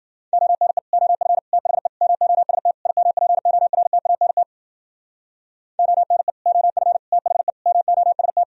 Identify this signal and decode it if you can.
Morse code, heard as a keyed tone, about 52 wpm, 700 Hz. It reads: ONE OF THE MOST IMPORTANT ONE OF THE MOST